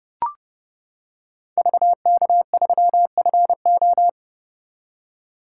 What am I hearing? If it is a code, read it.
Morse code, heard as a keyed tone, about 29 wpm, 700 Hz. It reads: VK3FO